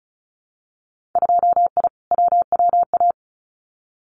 Morse code, heard as a keyed tone, about 34 wpm, 700 Hz. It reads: 2I WWA
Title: Winter Field Day Exchanges Encore 35wpm